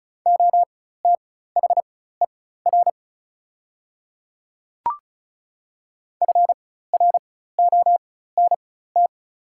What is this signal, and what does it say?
Morse code, heard as a keyed tone, about 36 wpm, spaced as for 16 wpm, 690 Hz. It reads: OTHER FRONT
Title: 4x Character Spacing - Top 500 Words - 35wpm